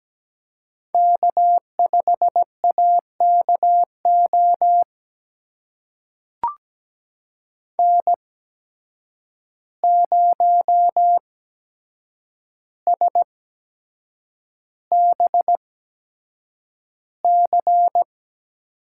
Morse code, heard as a keyed tone, about 17 wpm, spaced as for 7 wpm, 700 Hz. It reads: K5AKO N0SBC
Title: USA Call Signs - 8x Character Spacing - 17wpm